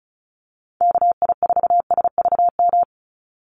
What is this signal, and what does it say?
Morse code, heard as a keyed tone, about 34 wpm, 700 Hz. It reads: KI4SVM
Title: NA SOTA Activator Callsigns 35wpm